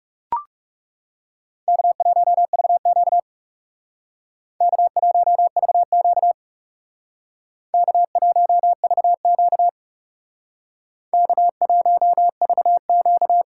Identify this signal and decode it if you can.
Morse code, heard as a keyed tone, about 37 wpm, 700 Hz. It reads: K1VQ K1VQ K1VQ K1VQ
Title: Speed Racing - USA 1x2 Call Signs 30wpm